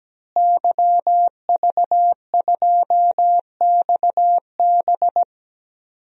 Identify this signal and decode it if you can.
Morse code, heard as a keyed tone, about 17 wpm, 700 Hz. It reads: YV2XB
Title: DXCC-100 Call Signs - 5 Characters - Repeated 2x - 17wpm